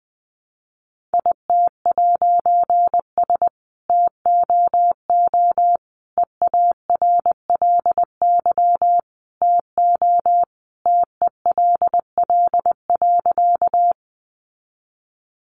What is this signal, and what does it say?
Morse code, heard as a keyed tone, about 20 wpm, 700 Hz. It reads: IT'S TOO EARLY TO TELL.
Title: A Sound Of Thunder 20wpm